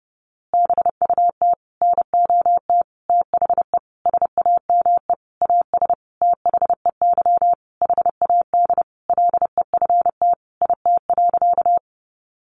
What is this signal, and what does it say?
Morse code, heard as a keyed tone, about 30 wpm, 700 Hz. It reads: BUT NOT THE SAME AS THEY HAD LEFT IT.
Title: A Sound Of Thunder 30wpm